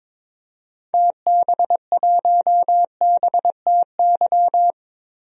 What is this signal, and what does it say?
Morse code, heard as a keyed tone, about 22 wpm, 700 Hz. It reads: TB1BTY
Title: DXCC-100 Call Signs - 6 Characters - 22wpm